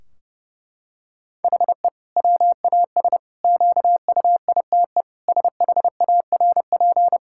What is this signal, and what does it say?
Morse code, heard as a keyed tone, about 30 wpm, 700 Hz. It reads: HE WAS QUITE SHARP
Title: Sentences from Top 1600 Words 30wpm